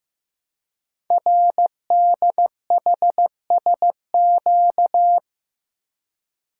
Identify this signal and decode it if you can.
Morse code, heard as a keyed tone, about 15 wpm, 700 Hz. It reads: RDHSQ
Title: Code Groups - 5 Letters - 15wpm